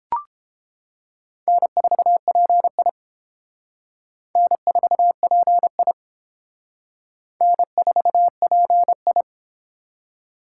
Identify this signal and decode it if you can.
Morse code, heard as a keyed tone, about 28 wpm, 700 Hz. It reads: N4PI N4PI N4PI